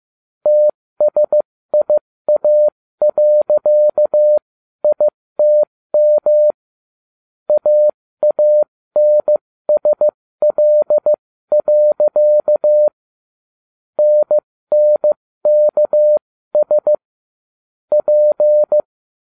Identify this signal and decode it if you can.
Morse code, heard as a keyed tone, about 15 wpm, spaced as for 13 wpm, 600 Hz. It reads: TSIA.ITM AANSL. NNKS P